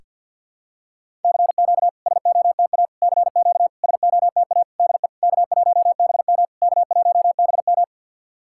Morse code, heard as a keyed tone, about 50 wpm, 690 Hz. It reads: CQ SOTA CQ SOTA DE K1BM K1BM
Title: SOTA QSO Practice 50wpm